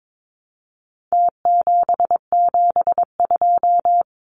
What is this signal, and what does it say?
Morse code, heard as a keyed tone, about 22 wpm, 700 Hz. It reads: T772
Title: Numbers - Contest Style - 4-Digit 22wpm